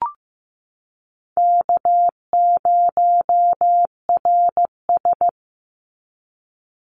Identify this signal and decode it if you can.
Morse code, heard as a keyed tone, about 15 wpm, 700 Hz. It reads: K0RS
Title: NA SOTA Chaser Callsigns 15wpm